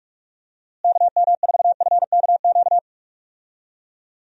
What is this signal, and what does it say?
Morse code, heard as a keyed tone, about 46 wpm, 700 Hz. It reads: KM4FKQ
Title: USA Call Signs - 6 Characters - Repeated 2x - 45wpm